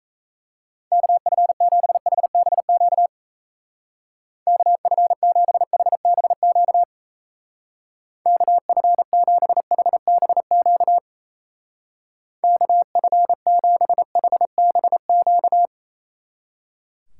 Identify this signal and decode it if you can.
Morse code, heard as a keyed tone, about 32 wpm, 700 Hz. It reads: KF7HBQ KF7HBQ KF7HBQ KF7HBQ
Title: Speed Racing - USA 2x3 Call Signs 28wpm